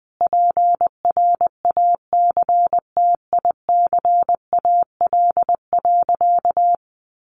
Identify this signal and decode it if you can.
Morse code, heard as a keyed tone, about 20 wpm, 700 Hz. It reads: PRACTICAL.
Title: Sentences from Top 1400 Words 20wpm